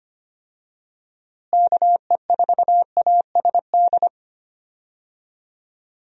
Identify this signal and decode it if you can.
Morse code, heard as a keyed tone, about 25 wpm, 700 Hz. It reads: KE4ASD
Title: USA Call Signs - 6 Characters - Repeated 2x - 25wpm